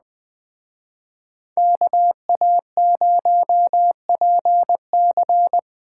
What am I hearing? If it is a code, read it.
Morse code, heard as a keyed tone, about 21 wpm, 700 Hz. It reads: KA0PC
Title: Speed Racing - USA 2x2 Call Signs 17wpm